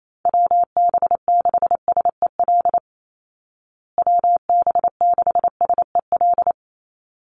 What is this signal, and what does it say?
Morse code, heard as a keyed tone, about 28 wpm, 690 Hz. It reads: WB6SEL WB6SEL
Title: Call Signs Repeated 3x - 28wpm